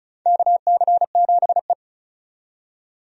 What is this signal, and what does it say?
Morse code, heard as a keyed tone, about 35 wpm, 700 Hz. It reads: KC7E
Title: USA Call Signs - 4 Characters - 35wpm